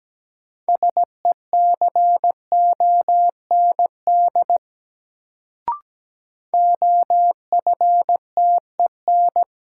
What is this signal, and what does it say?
Morse code, heard as a keyed tone, about 17 wpm, 700 Hz. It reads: SECOND OFTEN
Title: Sentences from Top 300 Words - New Word Drill - 17wpm